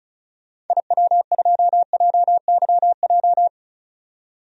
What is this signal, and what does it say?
Morse code, heard as a keyed tone, about 34 wpm, 700 Hz. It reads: IW2JYJ
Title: DXCC-100 Call Signs - 6 Characters - 35wpm